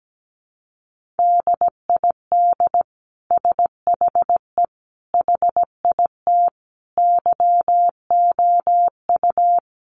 Morse code, heard as a keyed tone, about 17 wpm, 700 Hz. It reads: DID SHE HIT YOU
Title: Sentences from Top 1400 Words 17wpm